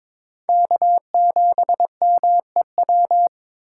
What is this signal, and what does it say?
Morse code, heard as a keyed tone, about 22 wpm, 700 Hz. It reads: K7MEW
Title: USA Call Signs - 5 Characters - 6x Character Spacing - 22wpm